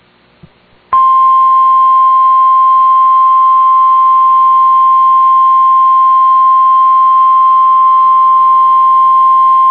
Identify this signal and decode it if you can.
Emergency Alert System (SAME) tones (the 1050 Hz attention tone). It is heard from 1.2 to 9.0 s.